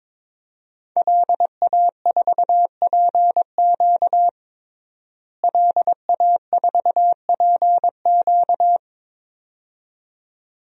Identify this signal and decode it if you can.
Morse code, heard as a keyed tone, about 22 wpm, 700 Hz. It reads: LA4PQ LA4PQ
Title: DXCC-100 Call Signs - 5 Characters - Repeated 2x - 22wpm